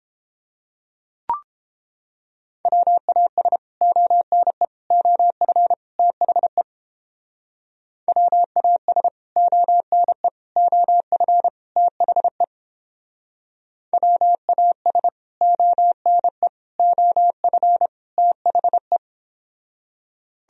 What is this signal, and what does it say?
Morse code, heard as a keyed tone, about 29 wpm, 700 Hz. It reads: WAS ONE OF THE WAS ONE OF THE WAS ONE OF THE